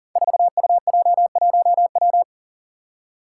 Morse code, heard as a keyed tone, about 41 wpm, 700 Hz. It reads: 4UJ1W
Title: Five Challenging Letter-Number Combinations 40wpm